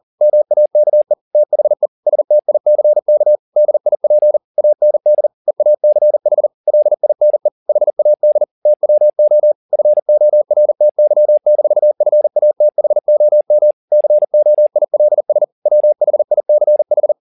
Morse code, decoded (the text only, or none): MAKE THE STICK DIP AND EACH LINE HAD TWO FORTY-FATHOM COILS WHICH